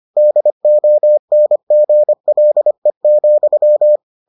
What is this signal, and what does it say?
Morse code, heard as a keyed tone, about 25 wpm, 600 Hz. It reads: DONGLE,